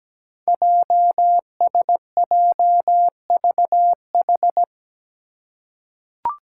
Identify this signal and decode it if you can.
Morse code, heard as a keyed tone, about 17 wpm, 700 Hz. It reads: JSJVH
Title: Five Challenging Letter-Number Combinations 17wpm